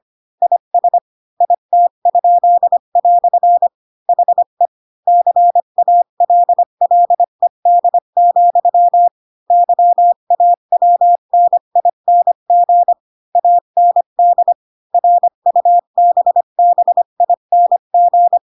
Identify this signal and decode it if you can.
Morse code, heard as a keyed tone, about 25 wpm, 700 Hz. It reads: IS IT?" HE CALLED, YAWNING AND RUBBING